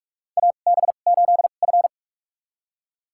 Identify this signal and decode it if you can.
Morse code, heard as a keyed tone, about 46 wpm, spaced as for 33 wpm, 700 Hz. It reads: AB8F
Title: USA Call Signs - 2x Character Spacing - 45wpm